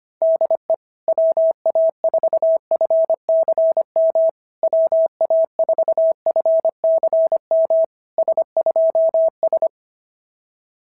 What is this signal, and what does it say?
Morse code, heard as a keyed tone, about 25 wpm, 650 Hz. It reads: DE WA4FCM WA4FCM S2S
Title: SOTA QSO Practice 25wpm